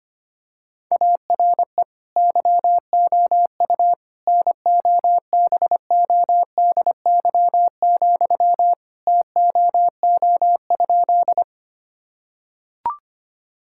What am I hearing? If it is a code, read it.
Morse code, heard as a keyed tone, about 25 wpm, 700 Hz. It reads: ARE YOU NOBODY, TOO?
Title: Sentences from Top 1600 Words 25wpm